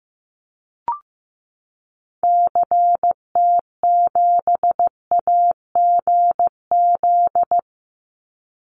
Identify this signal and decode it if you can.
Morse code, heard as a keyed tone, about 15 wpm, 700 Hz. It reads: CT7AGZ
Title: Call Signs Repeated 2x - 15wpm